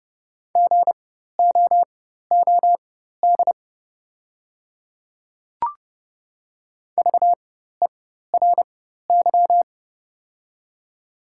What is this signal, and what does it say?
Morse code, heard as a keyed tone, about 29 wpm, spaced as for 14 wpm, 700 Hz. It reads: GOOD VERY
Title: 4x Character Spacing - Top 500 Words - 30wpm